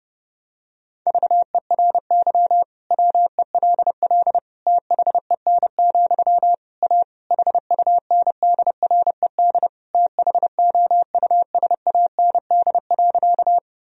Morse code, heard as a keyed tone, about 32 wpm, 700 Hz. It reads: VERY WELL THEN, A HUNDRED THOUSAND.
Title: Sentences from Top 1000 Words - Review Part 3 - 30wpm